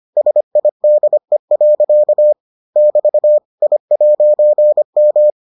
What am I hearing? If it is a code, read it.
Morse code, heard as a keyed tone, about 25 wpm, 600 Hz. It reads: SIDE. <BT> I'M